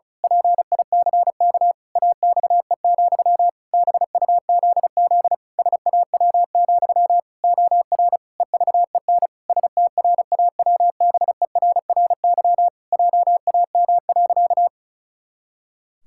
Morse code, heard as a keyed tone, about 34 wpm, 700 Hz. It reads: PICK AXE, BUZZ SAW, OR EVEN STRAWBERRY JAM.
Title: Pangram Challenge v2 - 35wpm